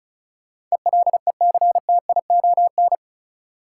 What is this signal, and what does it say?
Morse code, heard as a keyed tone, about 34 wpm, 700 Hz. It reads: ELECTION
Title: Sentences from Top 1500 Words - New Word Drill Alt - 35wpm